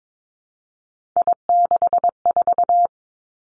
Morse code, heard as a keyed tone, about 22 wpm, 700 Hz. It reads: I64
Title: Three Challenging Letter-Number Combinations 22wpm